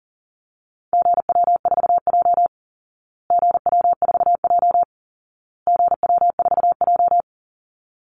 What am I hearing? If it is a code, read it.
Morse code, heard as a keyed tone, about 40 wpm, 700 Hz. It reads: GW4J GW4J GW4J
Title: Call Signs Repeated 2x - 40wpm